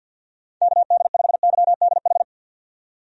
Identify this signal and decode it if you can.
Morse code, heard as a keyed tone, about 52 wpm, 700 Hz. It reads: KD5YDH
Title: USA Call Signs - 6 Characters - 50wpm